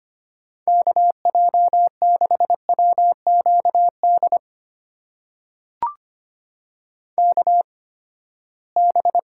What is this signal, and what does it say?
Morse code, heard as a keyed tone, about 25 wpm, spaced as for 17 wpm, 700 Hz. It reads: KJ6WQD KB
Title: USA Call Signs - 6 Characters - 8x Character Spacing - 25wpm